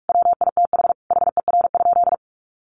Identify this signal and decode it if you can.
Morse code, heard as a keyed tone, about 44 wpm, 700 Hz. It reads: WITH HER?
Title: Sentences from Top 1300 Words 45wpm